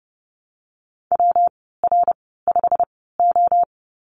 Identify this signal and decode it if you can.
Morse code, heard as a keyed tone, about 31 wpm, spaced as for 17 wpm, 700 Hz. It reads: WR5O
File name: Lesson 19g - Letter w - Call Signs 30wpm